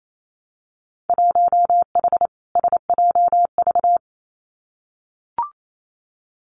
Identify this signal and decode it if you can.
Morse code, heard as a keyed tone, about 28 wpm, 700 Hz. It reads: 1H SJV
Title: Winter Field Day Exchanges 28wpm